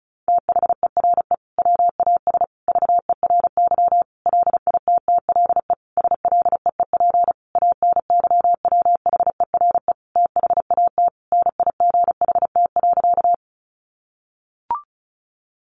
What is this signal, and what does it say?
Morse code, heard as a keyed tone, about 34 wpm, 700 Hz. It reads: THERE WAS VERY LITTLE SLEEP ANYWHERE THAT NIGHT.
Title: Sentences from Top 1400 Words 35wpm